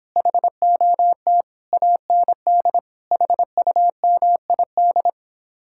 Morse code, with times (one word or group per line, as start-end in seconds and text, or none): 0.2-1.4 HOT
1.7-2.8 AND
3.1-5.1 HUMID